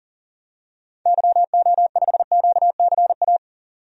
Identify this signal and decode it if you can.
Morse code, heard as a keyed tone, about 40 wpm, 700 Hz. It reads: YO5QCA
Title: DXCC-100 Call Signs - 6 Characters - 40wpm